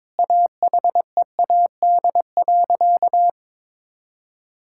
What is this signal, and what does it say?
Morse code, heard as a keyed tone, about 22 wpm, 700 Hz. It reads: AHEAD.
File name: practice / 22wpm / Sentences with 2-Word Phrases - 22wpm